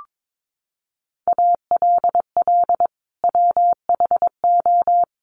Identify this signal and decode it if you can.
Morse code, heard as a keyed tone, about 22 wpm, 700 Hz. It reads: ALL WHO
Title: Sentences from Top 1400 Words 22wpm